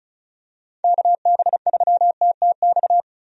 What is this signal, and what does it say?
Morse code, heard as a keyed tone, about 36 wpm, 700 Hz. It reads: KB3TTX
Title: USA Call Signs - 6 Characters - 35wpm